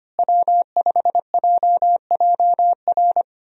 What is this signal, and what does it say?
Morse code, heard as a keyed tone, about 25 wpm, 700 Hz. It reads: W5JJR
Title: USA Call Signs - 5 Characters - Repeated 3x - 25wpm